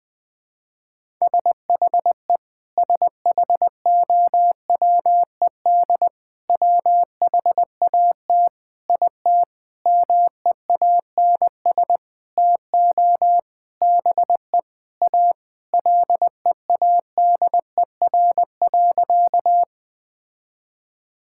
Morse code, heard as a keyed tone, about 20 wpm, 700 Hz. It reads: SHE SHOWED WHAT IT MEANS TO BE A LEADER.
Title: Sentences with 6-Word Phrases - Spoken Beginning - 20wpm